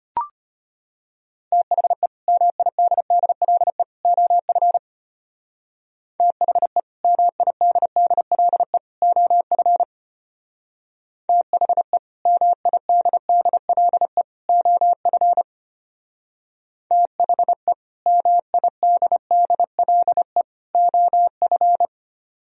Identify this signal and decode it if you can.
Morse code, heard as a keyed tone, about 33 wpm, 700 Hz. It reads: THE MIDDLE OF THE MIDDLE OF THE MIDDLE OF THE MIDDLE OF